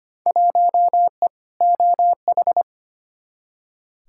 Morse code, heard as a keyed tone, about 25 wpm, 700 Hz. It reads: 1E OH
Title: Field Day Exchanges - 25wpm